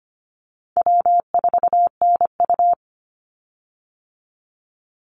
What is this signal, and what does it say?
Morse code, heard as a keyed tone, about 25 wpm, 700 Hz. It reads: W4NU
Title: Easy Call Signs - 25wpm